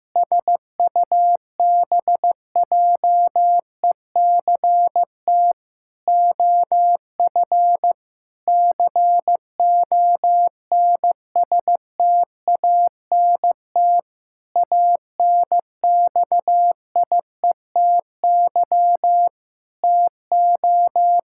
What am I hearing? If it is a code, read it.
Morse code, heard as a keyed tone, about 15 wpm, 700 Hz. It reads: SUBJECT OF CONSTANT ANXIETY TO
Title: Sentences from Top 1600 Words 15wpm